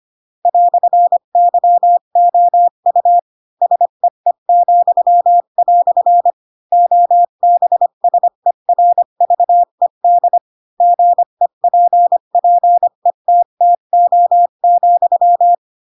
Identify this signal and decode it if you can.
Morse code, heard as a keyed tone, about 25 wpm, 700 Hz. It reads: "YOU SEE," OBSERVED GEPPETTO,
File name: Pinocchio / Pinocchio ch7 20 25 ik1hge